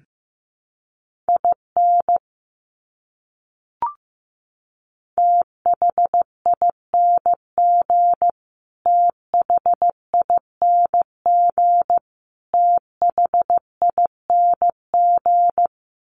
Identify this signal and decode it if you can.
Morse code, heard as a keyed tone, about 15 wpm, 700 Hz. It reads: IN THING THING THING
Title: Top 200 Words Repeated 15wpm